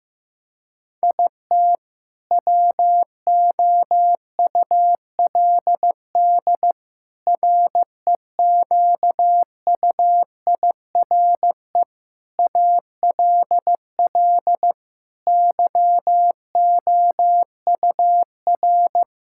Morse code, heard as a keyed tone, about 15 wpm, 700 Hz. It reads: IT WOULD REQUIRE ALL YOUR
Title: Sentences from Top 1000 Words - Review Part 6 - 15wpm